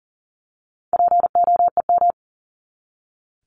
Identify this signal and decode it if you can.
Morse code, heard as a keyed tone, about 40 wpm, 700 Hz. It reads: POEM
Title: Top 1751-2000 Words 40wpm